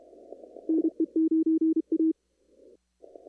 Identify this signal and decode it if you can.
Morse code, heard as a keyed tone, about 32 wpm, 320 Hz. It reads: NE9A
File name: CW / kt5j wpx cw-op-k5pi